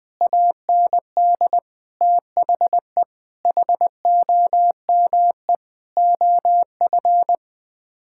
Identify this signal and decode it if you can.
Morse code, heard as a keyed tone, about 20 wpm, 700 Hz. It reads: AND THE HOME OF
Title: Sets of 9 Words - 20wpm